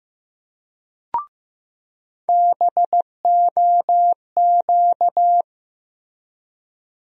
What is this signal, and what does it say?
Morse code, heard as a keyed tone, about 15 wpm, 700 Hz. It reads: BOQ
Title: USA Call Sign Postfixes - 3 Letters 15wpm